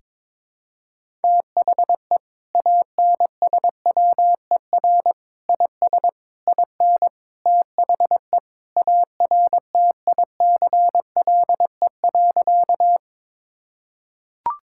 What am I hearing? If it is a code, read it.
Morse code, heard as a keyed tone, about 22 wpm, 700 Hz. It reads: THE ANSWER IS IN THE ARTICLE.
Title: Sentences from Top 1000 Words - Review Part 5 - 22wpm